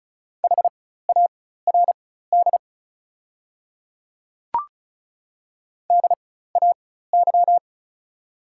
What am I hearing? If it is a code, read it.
Morse code, heard as a keyed tone, about 36 wpm, spaced as for 16 wpm, 700 Hz. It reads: HARD DAY
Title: 4x Character Spacing - Top 300 Words - 35wpm